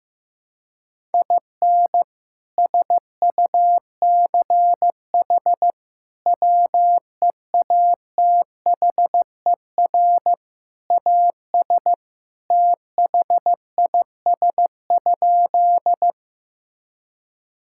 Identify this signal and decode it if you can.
Morse code, heard as a keyed tone, about 15 wpm, 700 Hz. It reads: IN SUCH WEATHER AS THIS?
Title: Sentences from Top 1000 Words - Review Part 5 - 15wpm